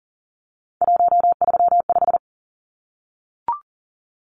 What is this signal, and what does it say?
Morse code, heard as a keyed tone, about 40 wpm, 700 Hz. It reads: 135
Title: Numbers - 3-Digit 40wpm